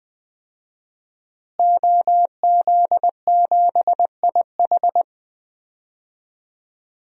Morse code, heard as a keyed tone, about 20 wpm, 700 Hz. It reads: OZ7IH
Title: DXCC-100 Call Signs - 5 Characters - Repeated 2x - 20wpm